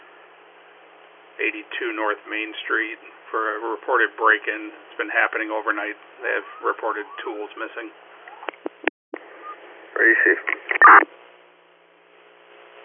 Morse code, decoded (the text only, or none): none